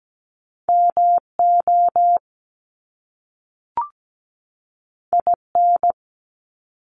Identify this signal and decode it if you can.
Morse code, heard as a keyed tone, about 17 wpm, 700 Hz. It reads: MO IN
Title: US States 17wpm